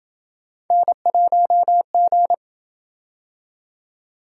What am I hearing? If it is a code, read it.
Morse code, heard as a keyed tone, about 26 wpm, 700 Hz. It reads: N1G